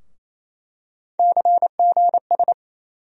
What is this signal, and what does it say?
Morse code, heard as a keyed tone, about 27 wpm, 700 Hz. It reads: CGS
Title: USA Call Sign Postfixes - 3 Letters 28wpm